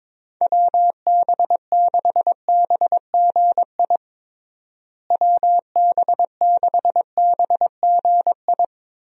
Morse code, heard as a keyed tone, about 22 wpm, 700 Hz. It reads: WB6BGI WB6BGI
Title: USA Call Signs - 6 Characters - Repeated 2x - 22wpm